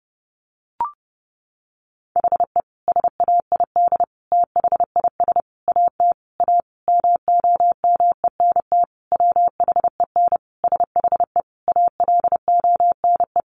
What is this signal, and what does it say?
Morse code, heard as a keyed tone, about 30 wpm, 700 Hz. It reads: HE SAID THIS AT A MOMENT WHEN SHE ALONE
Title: Sentences from Top 1000 Words 30wpm